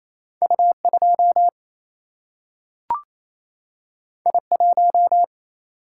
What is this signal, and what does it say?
Morse code, heard as a keyed tone, about 28 wpm, 700 Hz. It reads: U 2 I 1
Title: Single-Letter Single-Number 28wpm